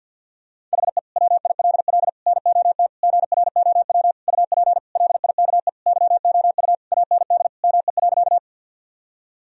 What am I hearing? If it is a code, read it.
Morse code, heard as a keyed tone, about 47 wpm, 700 Hz. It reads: HE WILL NOT GROW UP LIKE YOU AND ME.
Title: Sentences from Top 1000 Words - Review Part 1 - 50wpm